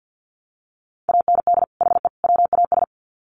Morse code, heard as a keyed tone, about 50 wpm, 700 Hz. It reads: AND HE WAS
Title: Sets of 3 Words 50wpm